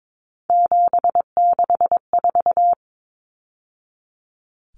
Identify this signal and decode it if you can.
Morse code, heard as a keyed tone, about 22 wpm, 700 Hz. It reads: 764
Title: Numbers - 3-Digit 22wpm